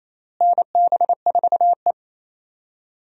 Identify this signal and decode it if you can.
Morse code, heard as a keyed tone, about 28 wpm, 700 Hz. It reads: NB4E
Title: USA Call Signs - 4 Characters - Repeated 2x - 28wpm